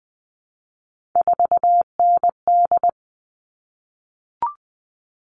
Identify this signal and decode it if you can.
Morse code, heard as a keyed tone, about 20 wpm, 700 Hz. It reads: 4ND
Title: All Cut Numbers - 3-Digit 20wpm